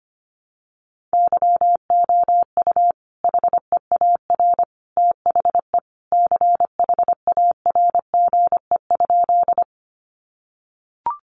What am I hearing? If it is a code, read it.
Morse code, heard as a keyed tone, about 25 wpm, 700 Hz. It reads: YOU HEAR THE CHARGE?
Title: Sentences from Top 900 Words 25wpm